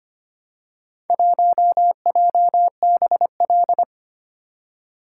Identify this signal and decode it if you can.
Morse code, heard as a keyed tone, about 25 wpm, 700 Hz. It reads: 1JBL
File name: practice / 25wpm / Four Letter-Number Combinations - 25wpm